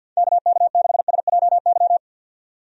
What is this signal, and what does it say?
Morse code, heard as a keyed tone, about 51 wpm, 700 Hz. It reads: KK6SJY
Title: USA Call Signs - 6 Characters - 50wpm